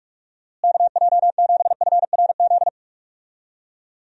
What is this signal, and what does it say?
Morse code, heard as a keyed tone, about 42 wpm, 700 Hz. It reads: KJ7FRZ
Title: USA Call Signs - 6 Characters - Repeated 3x - 45wpm